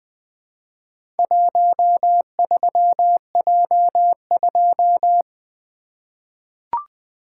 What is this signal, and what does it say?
Morse code, heard as a keyed tone, about 20 wpm, 700 Hz. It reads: 13J2
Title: Four Letter-Number Combinations - 20wpm